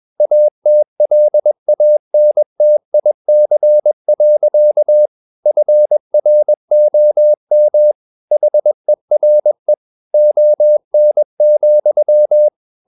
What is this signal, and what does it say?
Morse code, heard as a keyed tone, about 21 wpm, 600 Hz. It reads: ATLANTIC. FROM HERE ON,
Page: CW med Gnister